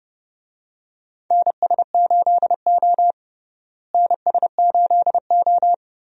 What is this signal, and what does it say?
Morse code, heard as a keyed tone, about 32 wpm, 700 Hz. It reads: NS8O NS8O
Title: USA Call Signs - 4 Characters - Repeated 2x - 30wpm